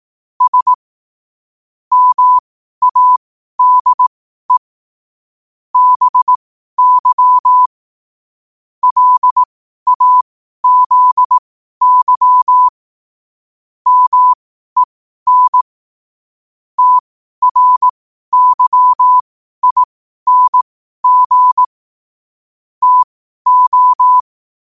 Morse code, heard as a keyed tone, about 18 wpm, spaced as for 12 wpm, 1000 Hz. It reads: S MADE BY LAZY MEN TRYING TO
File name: qotdfiles / 12wpm